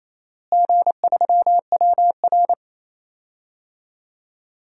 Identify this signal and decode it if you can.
Morse code, heard as a keyed tone, about 28 wpm, 700 Hz. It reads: G3WR
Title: DXCC-100 Call Signs - 4 Characters - Repeated 3x - 28wpm